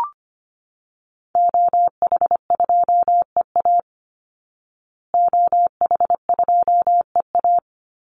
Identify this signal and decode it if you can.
Morse code, heard as a keyed tone, about 25 wpm, 700 Hz. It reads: OH2EA OH2EA
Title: Call Signs Repeated 3x - 25wpm